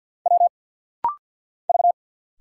Morse code, heard as a keyed tone, about 50 wpm, 700 Hz. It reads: W V
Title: Single Letters - Rapid-Fire 50wpm